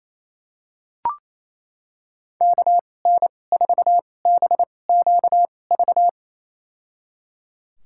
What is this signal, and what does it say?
Morse code, heard as a keyed tone, about 28 wpm, spaced as for 20 wpm, 700 Hz. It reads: KN4BQV
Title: USA Call Signs - 2x Character Spacing - 28wpm